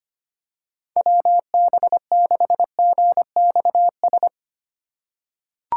Morse code, heard as a keyed tone, about 25 wpm, 700 Hz. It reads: WB6GXS
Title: USA Call Signs - 6 Characters - 4x Character Spacing - 25wpm